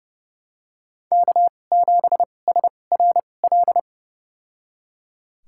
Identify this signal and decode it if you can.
Morse code, heard as a keyed tone, about 30 wpm, spaced as for 22 wpm, 700 Hz. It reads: K7SRL
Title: USA Call Signs - 5 Characters - 2x Character Spacing - 30wpm